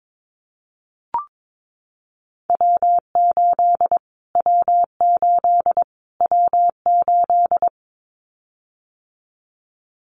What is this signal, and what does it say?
Morse code, heard as a keyed tone, about 22 wpm, 700 Hz. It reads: W8 W8 W8
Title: Callsign Prefixes for Top 25 DXCC - Repeated 3x - 22wpm